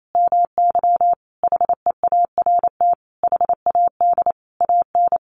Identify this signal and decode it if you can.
Morse code, heard as a keyed tone, about 28 wpm, 700 Hz. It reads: MY HEART HAD AN
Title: Sentences from Top 1200 Words 28wpm